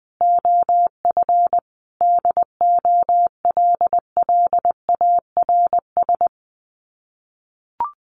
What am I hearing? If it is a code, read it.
Morse code, heard as a keyed tone, about 19 wpm, 700 Hz. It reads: OF DOLLARS
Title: Sets of 5 Words Encore 20wpm